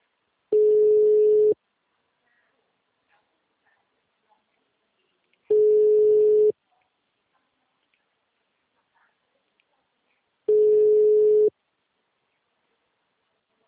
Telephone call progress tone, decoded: ringback tone

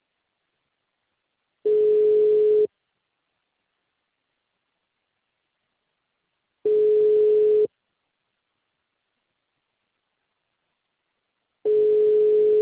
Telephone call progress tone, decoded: ringback tone

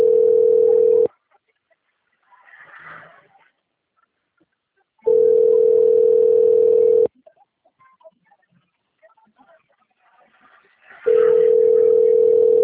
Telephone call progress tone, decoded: ringback tone

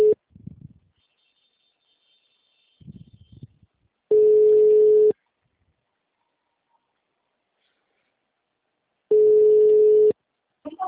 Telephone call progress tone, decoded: ringback tone

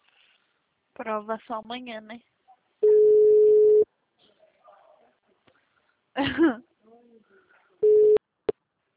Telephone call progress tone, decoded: ringback tone

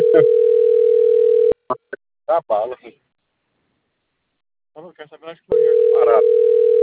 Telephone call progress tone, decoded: ringback tone

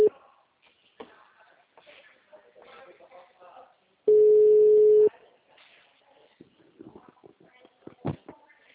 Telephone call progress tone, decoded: ringback tone

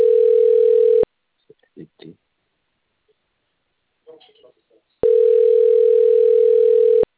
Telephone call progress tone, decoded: ringback tone

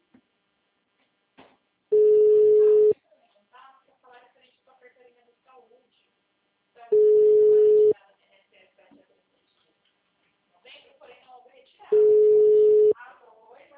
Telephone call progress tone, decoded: ringback tone

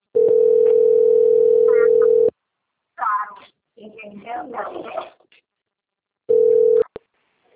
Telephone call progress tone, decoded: ringback tone

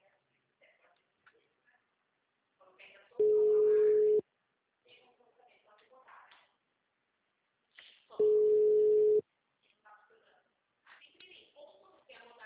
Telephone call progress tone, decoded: ringback tone

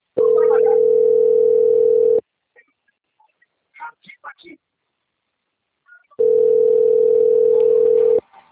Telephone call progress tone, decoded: ringback tone

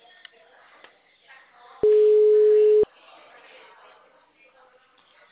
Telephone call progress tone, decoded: ringback tone

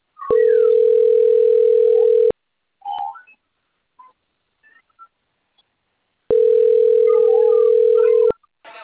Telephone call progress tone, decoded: ringback tone

